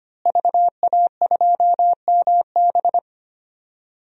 Morse code, heard as a keyed tone, about 25 wpm, 700 Hz. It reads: VA2MB
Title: DXCC-100 Call Signs - 5 Characters - Repeated 3x - 25wpm